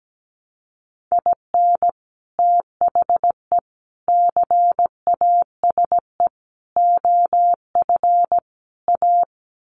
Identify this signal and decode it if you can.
Morse code, heard as a keyed tone, about 17 wpm, 700 Hz. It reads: IN THE CASE OF A
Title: Sets of 5 Words Encore 17wpm